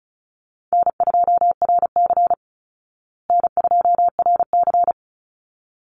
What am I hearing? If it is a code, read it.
Morse code, heard as a keyed tone, about 34 wpm, 700 Hz. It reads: N2RC N2RC
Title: Easy Call Signs Repeated 2x - 35wpm